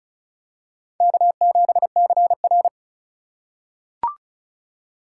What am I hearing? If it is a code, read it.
Morse code, heard as a keyed tone, about 36 wpm, 700 Hz. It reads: K7CR
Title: USA Call Signs - 4 Characters - 3x Character Spacing - 35wpm